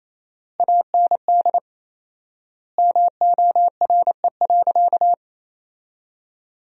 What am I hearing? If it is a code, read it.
Morse code, heard as a keyed tone, about 28 wpm, 700 Hz. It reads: AND MORE.
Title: 4x Word-Spacing - Sentences from Top 500 Words - 28wpm